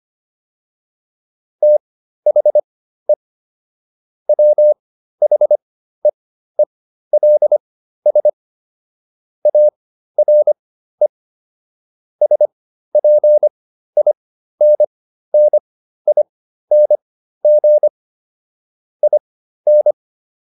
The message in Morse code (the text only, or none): THE WHEELS ARE SPINNING IN